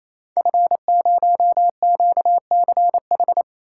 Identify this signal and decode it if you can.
Morse code, heard as a keyed tone, about 29 wpm, 700 Hz. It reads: F0QCH
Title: Code Groups - 5 Letter:Numbers - 28wpm